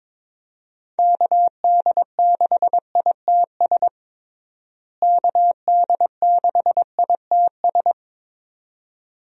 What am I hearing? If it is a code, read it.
Morse code, heard as a keyed tone, about 22 wpm, 700 Hz. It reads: KD6ITS KD6ITS
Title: USA Call Signs - 6 Characters - Repeated 2x - 22wpm